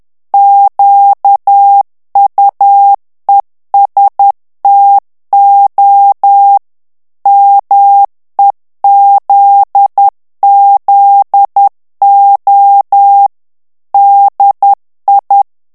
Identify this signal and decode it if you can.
Morse code, heard as a keyed tone, about 10 wpm, 800 Hz. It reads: QUESTO MEZZO DI
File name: morse mp3 / ITALIEN texte clair 1 vitesse 10 mots minute